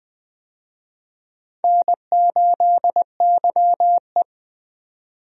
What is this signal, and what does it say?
Morse code, heard as a keyed tone, about 20 wpm, 700 Hz. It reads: N8YE
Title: USA Call Signs - 4 Characters - 20wpm